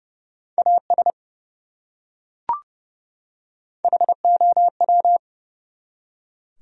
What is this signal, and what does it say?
Morse code, heard as a keyed tone, about 30 wpm, 700 Hz. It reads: AS HOW
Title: Top 100 Words 30wpm